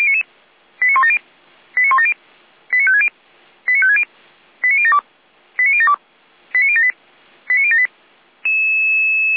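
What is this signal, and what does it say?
Five-tone selective call (ZVEI), heard as quick runs of five tones, tones of about 70 ms. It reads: 89170 89170 89560 89560 89072 89072 89078 89078
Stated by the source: Five-tone ZVEI selective call 89150, received by SDR.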